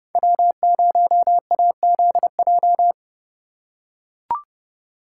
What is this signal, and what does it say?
Morse code, heard as a keyed tone, about 29 wpm, 700 Hz. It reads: W0AZJ